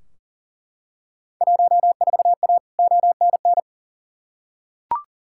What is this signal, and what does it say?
Morse code, heard as a keyed tone, about 40 wpm, 700 Hz. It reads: 14A ONN
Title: Field Day Exchanges Encore - 40wpm